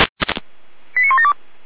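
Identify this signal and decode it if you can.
Five-tone selective call (ZVEI), heard as quick runs of five tones, tones of about 70 ms. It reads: 89172
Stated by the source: ZVEI five-tone call received by SDR.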